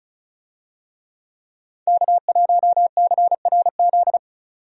Morse code, heard as a keyed tone, about 34 wpm, 700 Hz. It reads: K1CRZ
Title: USA Call Signs - 5 Characters - 35wpm